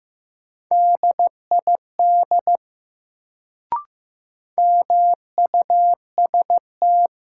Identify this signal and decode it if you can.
Morse code, heard as a keyed tone, about 15 wpm, 700 Hz. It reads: DID MUST
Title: Sets of 2 Words - New Word Drill - 15wpm